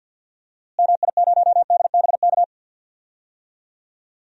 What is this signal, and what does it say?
Morse code, heard as a keyed tone, about 51 wpm, 700 Hz. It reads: MI0DBK
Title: DXCC-100 Call Signs - 6 Characters - Repeated 2x - 50wpm